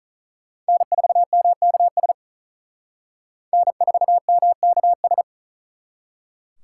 Morse code, heard as a keyed tone, about 45 wpm, 700 Hz. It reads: N4MKS N4MKS